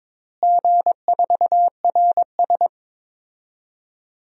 Morse code, heard as a keyed tone, about 22 wpm, 700 Hz. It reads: G4RS
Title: DXCC-100 Call Signs - 4 Characters - Repeated 2x - 22wpm